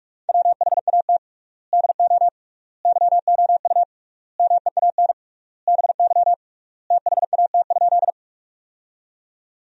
Morse code, heard as a keyed tone, about 46 wpm, 700 Hz. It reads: WHAT DO YOU MEAN BY THAT?
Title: 3x Word-Spacing - Sentences from Top 200 Words - 45wpm